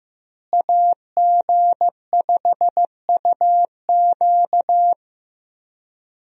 Morse code, heard as a keyed tone, about 15 wpm, 700 Hz. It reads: AG5UQ
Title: USA Call Signs - 5 Characters - Repeated 3x - 15wpm